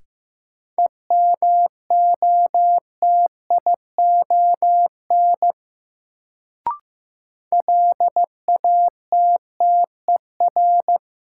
Morse code, heard as a keyed tone, about 15 wpm, 700 Hz. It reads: EMOTION LATTER
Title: Sentences from Top 1600 Words - New Word Drill - 15wpm